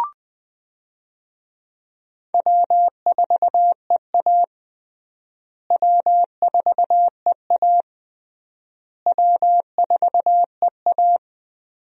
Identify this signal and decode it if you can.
Morse code, heard as a keyed tone, about 21 wpm, 700 Hz. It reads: W4EA W4EA W4EA
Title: USA Call Signs - 4 Characters - Repeated 3x - 20wpm